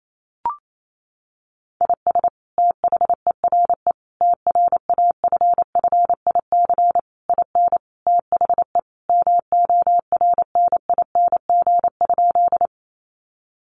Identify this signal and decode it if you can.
Morse code, heard as a keyed tone, about 28 wpm, 700 Hz. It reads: IS THERE TRAFFIC IN THE MORNING?
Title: Sentences from Top 1300 Words 28wpm